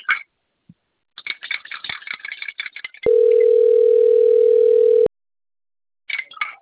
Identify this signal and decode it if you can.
Ringback tone (North America), 440 + 480 Hz.